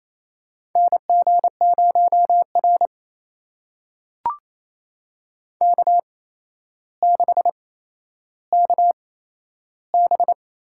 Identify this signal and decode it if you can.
Morse code, heard as a keyed tone, about 28 wpm, spaced as for 12 wpm, 700 Hz. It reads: NG0R K6KB
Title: USA Call Signs - 8x Character Spacing - 28wpm